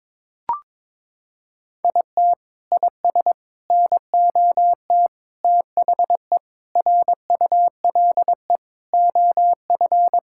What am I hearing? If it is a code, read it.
Morse code, heard as a keyed tone, about 22 wpm, 700 Hz. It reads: IT IS NOT THE RULE OF
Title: Sentences from Top 1000 Words - Review Part 5 - 22wpm